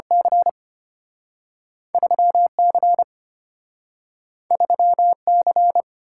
Morse code, heard as a keyed tone, about 34 wpm, 700 Hz. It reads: C 3C 3C